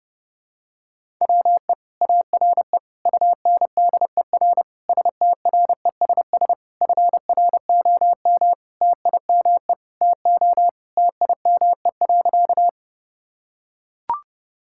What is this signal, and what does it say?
Morse code, heard as a keyed tone, about 30 wpm, 700 Hz. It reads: WE ARE UNDER STRESS FROM TIME TO TIME.
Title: Sentences from Top 1000 Words - Review Part 6 - 30wpm